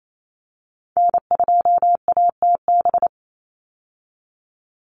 Morse code, heard as a keyed tone, about 28 wpm, 700 Hz. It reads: N2ATB
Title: Call Signs Repeated 3x - 28wpm